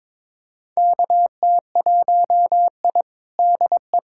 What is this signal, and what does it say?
Morse code, heard as a keyed tone, about 22 wpm, 700 Hz. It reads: KT1I DE
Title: USA Callsign DE Callsign - 4 Characters - 22wpm